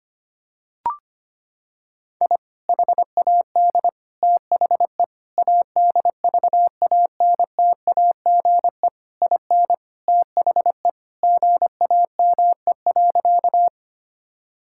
Morse code, heard as a keyed tone, about 25 wpm, 700 Hz. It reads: I HAD THE ADVANTAGE IN THE GAME.
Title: Sentences from Top 1000 Words - Review Part 6 - 25wpm